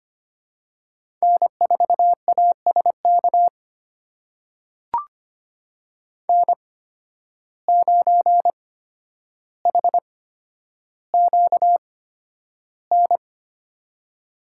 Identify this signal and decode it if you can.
Morse code, heard as a keyed tone, about 25 wpm, spaced as for 11 wpm, 700 Hz. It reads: N4ASK N9HQN
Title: USA Call Signs - 5 Characters - 8x Character Spacing - 25wpm